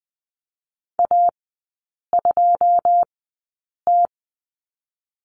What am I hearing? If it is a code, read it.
Morse code, heard as a keyed tone, about 20 wpm, 700 Hz. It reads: A 2 T
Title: 121 Lesson 20e - Number 3 - Three Characters 20wpm